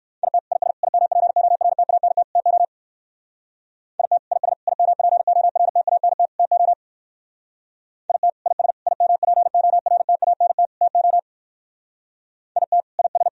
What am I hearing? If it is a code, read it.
Morse code, heard as a keyed tone, about 58 wpm, 700 Hz. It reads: IT IS IMPORTANT TO IT IS IMPORTANT TO IT IS IMPORTANT TO IT IS